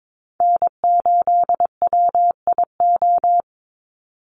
Morse code, heard as a keyed tone, about 22 wpm, 700 Hz. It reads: N8WIO
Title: USA Call Signs 22wpm